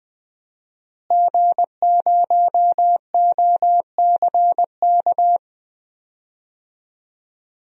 Morse code, heard as a keyed tone, about 20 wpm, 700 Hz. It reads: G0OCK
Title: DXCC-100 Call Signs - 5 Characters - Repeated 2x - 20wpm